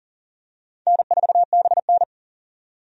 Morse code, heard as a keyed tone, about 40 wpm, 700 Hz. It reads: N4BN
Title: USA Call Signs - 4 Characters - 40wpm